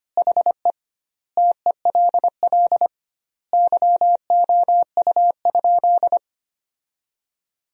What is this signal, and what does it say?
Morse code, heard as a keyed tone, about 25 wpm, 700 Hz. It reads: HE TELL YOU?
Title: 2x Word-Spacing - Sentences from Top 200 Words - 25wpm